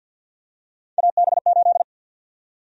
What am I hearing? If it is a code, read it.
Morse code, heard as a keyed tone, about 48 wpm, 700 Hz. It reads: AB8
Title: USA Call Sign Prefixes - 50wpm